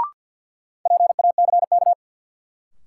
Morse code, heard as a keyed tone, about 53 wpm, 700 Hz. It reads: PACK
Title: Sentences from Top 1300 Words - New Word Drill Alt - 50wpm